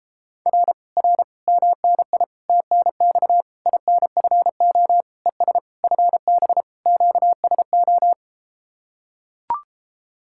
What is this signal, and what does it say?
Morse code, heard as a keyed tone, about 33 wpm, 700 Hz. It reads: R R MNI TNX INFO ES FB QSO